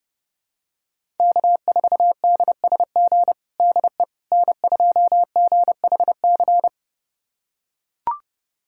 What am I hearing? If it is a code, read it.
Morse code, heard as a keyed tone, about 31 wpm, 700 Hz. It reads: K4DSG DE N2GHC
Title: USA Callsign DE Callsign - 5 Characters - 30wpm